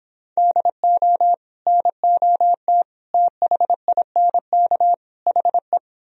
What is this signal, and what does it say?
Morse code, heard as a keyed tone, about 26 wpm, 700 Hz. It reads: DO NOT THINK HE